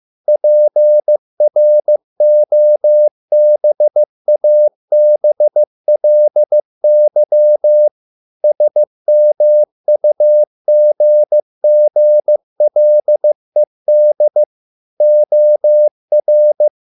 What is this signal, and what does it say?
Morse code, heard as a keyed tone, about 15 wpm, 600 Hz. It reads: PROBABLY SMUGGLED OR